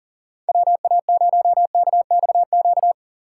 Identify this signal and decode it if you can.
Morse code, heard as a keyed tone, about 39 wpm, 700 Hz. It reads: WA0KXQ
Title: USA Call Signs - 4x Character Spacing - 40wpm